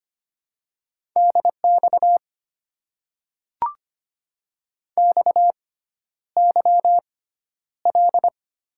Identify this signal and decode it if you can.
Morse code, heard as a keyed tone, about 25 wpm, spaced as for 11 wpm, 690 Hz. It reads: DX XYL
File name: Top 90 QSO Elements - 6x Character Spacing - 25wpm